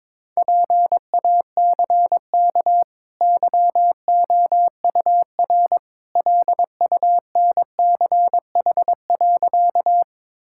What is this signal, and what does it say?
Morse code, heard as a keyed tone, about 22 wpm, 690 Hz. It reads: PACK YOUR LUNCH.
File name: practice / 22wpm / Easy Sentences with Hints - 22wpm